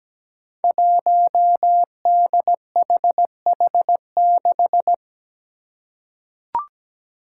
Morse code, heard as a keyed tone, about 17 wpm, 700 Hz. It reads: 1DHH6
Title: Five Challenging Letter-Number Combinations 17wpm